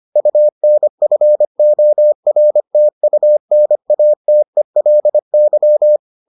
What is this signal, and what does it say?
Morse code, heard as a keyed tone, about 25 wpm, 600 Hz. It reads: UNFORTUNATELY